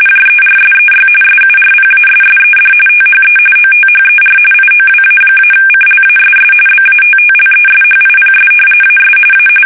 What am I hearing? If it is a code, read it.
RTTY, about 45.45 baud. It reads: JERSEY SURPASSED THE OTHER IOWA-CLASS BATTLESHIPS AS A
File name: fsk850 / battleshipnj